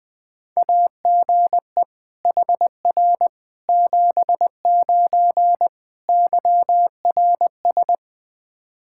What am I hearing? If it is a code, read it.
Morse code, heard as a keyed tone, about 20 wpm, 700 Hz. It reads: AGE HR 79 YRS
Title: Practice QSO - Exchange 3 - More Info - 20wpm